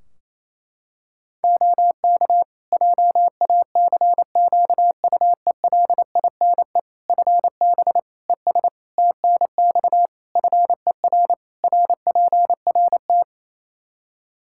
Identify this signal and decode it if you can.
Morse code, heard as a keyed tone, about 27 wpm, 700 Hz. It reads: OK JACQUELINE FB ES TNX FER RPRT
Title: K1USN SST QSO Practice 28wpm